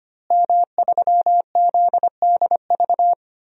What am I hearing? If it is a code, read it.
Morse code, heard as a keyed tone, about 25 wpm, 700 Hz. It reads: M3ZDV
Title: DXCC-100 Call Signs - 5 Characters - Repeated 3x - 25wpm